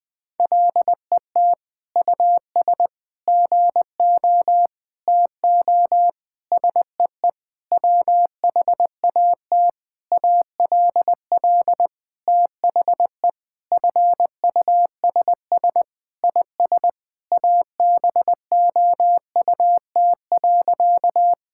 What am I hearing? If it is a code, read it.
Morse code, heard as a keyed tone, about 21 wpm, 700 Hz. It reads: LET US GO TO SEE WHAT ALL THE FUSS IS ABOUT.
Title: Sentences with 8-Word Phrases - Spoken Beginning - 20wpm